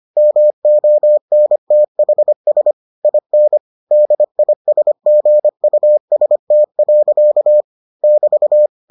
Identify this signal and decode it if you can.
Morse code, heard as a keyed tone, about 25 wpm, 600 Hz. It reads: MONTHS IN DISGUST. <BT>